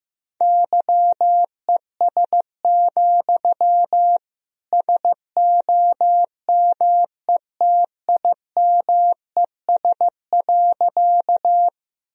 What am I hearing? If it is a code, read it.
Morse code, heard as a keyed tone, about 15 wpm, 700 Hz. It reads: YES, SOMETIMES.
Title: Sentences from Top 1000 Words - Review Part 3 - 15wpm